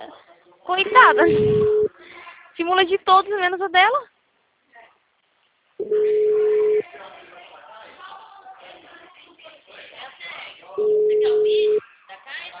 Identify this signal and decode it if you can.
Ringback tone (Europe, 425 Hz), with speech.
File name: out- -2006- - - .9055